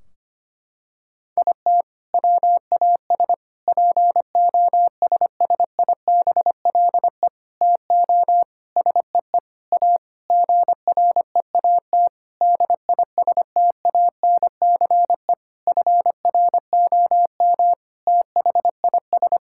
Morse code, heard as a keyed tone, about 25 wpm, 700 Hz. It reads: IT WAS POSSIBLE TO SEE A GREAT DISTANCE FROM THIS
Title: Sentences from Top 1600 Words 25wpm